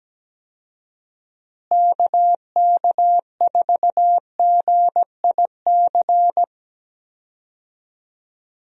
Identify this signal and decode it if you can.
Morse code, heard as a keyed tone, about 17 wpm, 700 Hz. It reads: KK4GIC
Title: USA Call Signs - 6 Characters - Repeated 2x - 17wpm